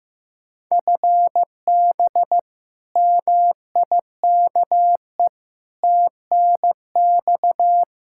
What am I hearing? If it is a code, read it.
Morse code, heard as a keyed tone, about 15 wpm, 700 Hz. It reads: FB MIKE TNX
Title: Practice QSO - Exchange 3 - More Info - 15wpm